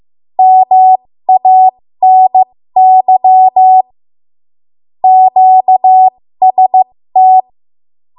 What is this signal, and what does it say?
Morse code, heard as a keyed tone, about 15 wpm, spaced as for 13 wpm, 750 Hz. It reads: MANY QST
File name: Morse / 13WPM